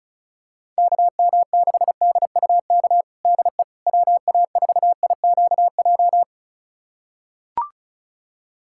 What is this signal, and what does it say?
Morse code, heard as a keyed tone, about 36 wpm, 700 Hz. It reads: KM6DUK DE WA4IQJ
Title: USA Callsign DE Callsign - 6 Characters - 35wpm